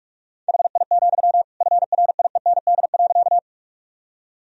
Morse code, heard as a keyed tone, about 46 wpm, 700 Hz. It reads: HI, FRIEND.
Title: Sentences from Top 1000 Words - Review Part 3 - 45wpm